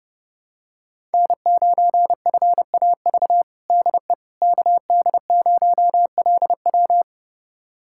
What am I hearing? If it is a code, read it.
Morse code, heard as a keyed tone, about 30 wpm, 700 Hz. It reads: N9FAV DE KD0LW
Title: USA Callsign DE Callsign - 5 Characters - 30wpm